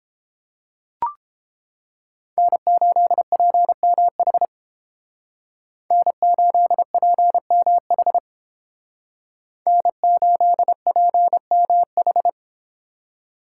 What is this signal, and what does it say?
Morse code, heard as a keyed tone, about 29 wpm, 700 Hz. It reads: N8PMH N8PMH N8PMH